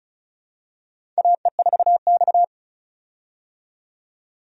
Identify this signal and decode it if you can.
Morse code, heard as a keyed tone, about 36 wpm, 700 Hz. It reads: AE4X
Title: USA Call Signs - 4 Characters - 35wpm